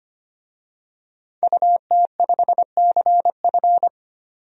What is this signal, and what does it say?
Morse code, heard as a keyed tone, about 25 wpm, 700 Hz. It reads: UT5CF
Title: DXCC-100 Call Signs - 5 Characters - 25wpm